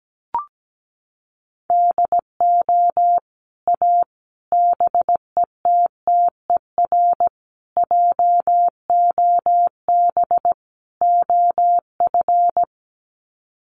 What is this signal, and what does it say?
Morse code, heard as a keyed tone, about 17 wpm, 700 Hz. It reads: DO A BETTER JOB OF
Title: Sets of 5 Words Encore2 17wpm